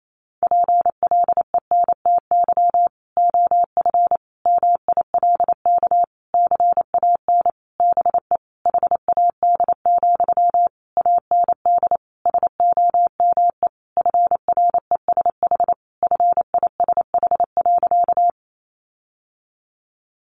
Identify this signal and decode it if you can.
Morse code, heard as a keyed tone, about 28 wpm, 700 Hz. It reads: PLENTY OF MILK CAN BE HAD, AND SOME FRESH FISH.
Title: Sentences from Top 1500 Words 28wpm